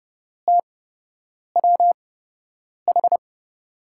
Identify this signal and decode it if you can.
Morse code, heard as a keyed tone, about 30 wpm, spaced as for 8 wpm, 700 Hz. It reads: TWH
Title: USA Call Signs - 5 Characters - 8x Character Spacing - 30wpm